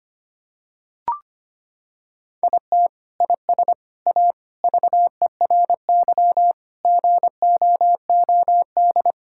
Morse code, heard as a keyed tone, about 25 wpm, 700 Hz. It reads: IT IS A VERY GOOD
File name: practice / 25wpm / Sentences with 3-Word Phrases - 25wpm